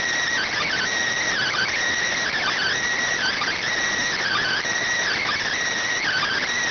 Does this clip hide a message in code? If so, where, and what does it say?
Yes: Emergency Alert System (SAME) tones (header bursts) from 0.8 to 5.6 s, heard as harsh buzzing bursts.